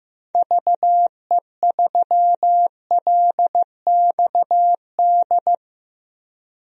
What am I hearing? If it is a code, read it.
Morse code, heard as a keyed tone, about 15 wpm, 700 Hz. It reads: VE3LXD
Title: DXCC-100 Call Signs - 6 Characters - Repeated 3x - 15wpm